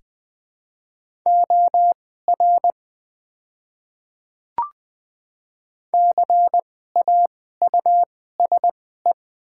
Morse code, heard as a keyed tone, about 20 wpm, spaced as for 14 wpm, 700 Hz. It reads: OR CAUSE
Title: 2x Character Spacing - Top 200 Words - 20wpm